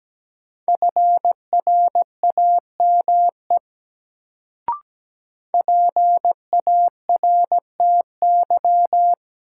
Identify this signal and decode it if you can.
Morse code, heard as a keyed tone, about 17 wpm, 700 Hz. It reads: FRAME PARTY
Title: Sentences from Top 1400 Words - New Word Drill Alt - 17wpm